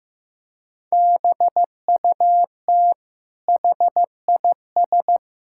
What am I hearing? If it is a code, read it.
Morse code, heard as a keyed tone, about 15 wpm, 700 Hz. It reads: BUT HIS
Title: Sentences from Top 1000 Words - Review Part 6 - 15wpm